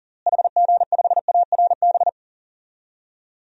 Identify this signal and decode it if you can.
Morse code, heard as a keyed tone, about 40 wpm, 700 Hz. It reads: HG5ARB
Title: DXCC-100 Call Signs - 6 Characters - Repeated 2x - 40wpm